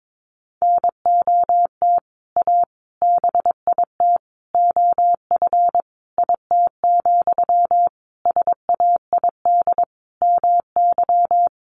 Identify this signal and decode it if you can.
Morse code, heard as a keyed tone, about 22 wpm, 700 Hz. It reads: NOT A BIT OF IT, SAID MY
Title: Sentences from Top 1400 Words 22wpm